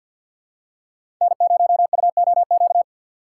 Morse code, heard as a keyed tone, about 51 wpm, 700 Hz. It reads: N0UOQ
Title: USA Call Signs - 5 Characters - 3x Character Spacing - 50wpm